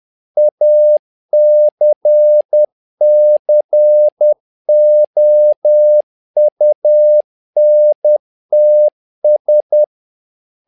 Morse code, heard as a keyed tone, about 10 wpm, 600 Hz. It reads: ACCOUNTS
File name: learning center / tower trouble / tower trouble 10wpm